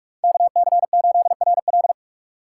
Morse code, heard as a keyed tone, about 46 wpm, 700 Hz. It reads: KC8RL